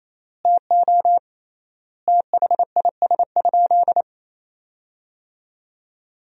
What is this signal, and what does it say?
Morse code, heard as a keyed tone, about 28 wpm, 700 Hz. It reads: TO THIS?
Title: 3x Word-Spacing - Sentences from Top 100 Words - 28wpm